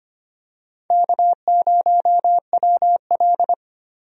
Morse code, heard as a keyed tone, about 25 wpm, 700 Hz. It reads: K0WL
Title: USA Call Signs - 4 Characters - Repeated 3x - 25wpm